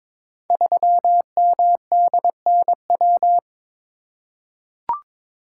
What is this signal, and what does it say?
Morse code, heard as a keyed tone, about 22 wpm, 700 Hz. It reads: 3MDNW
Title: Code Groups - 5 Letter:Numbers - 22wpm